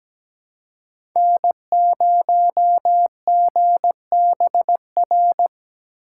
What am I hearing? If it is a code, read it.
Morse code, heard as a keyed tone, about 17 wpm, 700 Hz. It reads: N0GBR
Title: USA Call Signs - 5 Characters - 3x Character Spacing - 17wpm